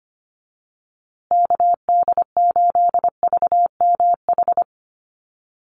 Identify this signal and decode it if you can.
Morse code, heard as a keyed tone, about 25 wpm, 700 Hz. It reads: KD8VMH
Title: USA Call Signs Repeated 3x 25wpm